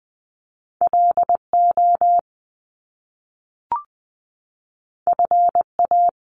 Morse code, heard as a keyed tone, about 20 wpm, 700 Hz. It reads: L O F A
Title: Top Two Letter Combinations in Words 20wpm